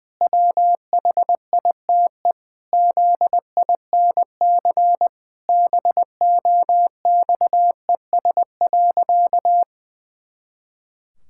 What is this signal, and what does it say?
Morse code, heard as a keyed tone, about 20 wpm, 700 Hz. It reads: WHITE ZINC BOXES.
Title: Pangram Challenge v2 - 20wpm